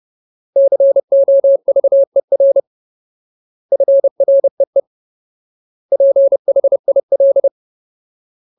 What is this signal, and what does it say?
Morse code, heard as a keyed tone, about 30 wpm, 550 Hz. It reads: COVER FREE PHIL
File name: Ultimate Word List / 30WPM Ultimate